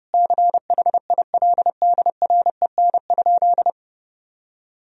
Morse code, heard as a keyed tone, about 30 wpm, 700 Hz. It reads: CHILDREN?
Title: Sentences from Top 1000 Words - Review Part 2 - 30wpm